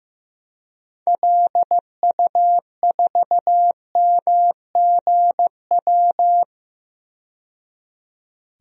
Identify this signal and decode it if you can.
Morse code, heard as a keyed tone, about 15 wpm, 700 Hz. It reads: LU4MGW
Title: DXCC-100 Call Signs - 6 Characters - 15wpm